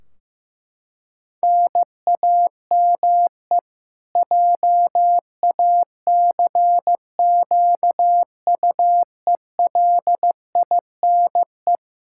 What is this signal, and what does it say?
Morse code, heard as a keyed tone, about 15 wpm, 700 Hz. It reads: NAME JACQUELINE
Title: Practice QSO - Exchange 1 - Beginning - 15wpm